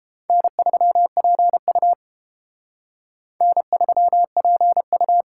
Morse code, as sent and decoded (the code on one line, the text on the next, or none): -. ...-- .--. ..- / -. ...-- .--. ..-
N3PU N3PU